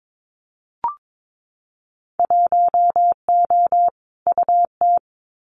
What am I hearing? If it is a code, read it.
Morse code, heard as a keyed tone, about 22 wpm, 700 Hz. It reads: 1O UT
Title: Winter Field Day Exchanges Encore 22wpm